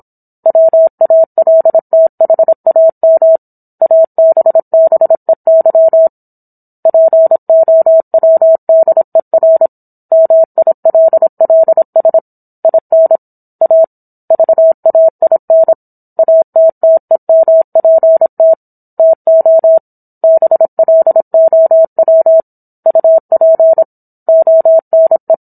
Morse code, heard as a keyed tone, about 27 wpm, 650 Hz. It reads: WALTHAM ABBEY POWDER MILLS IN A VAIN ATTEMPT TO BLOW UP ONE